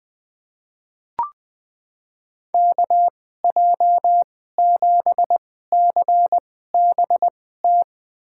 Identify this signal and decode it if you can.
Morse code, heard as a keyed tone, about 20 wpm, spaced as for 15 wpm, 700 Hz. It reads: KJ7CBT
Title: USA Call Signs - 6 Characters - 2x Character Spacing - 20wpm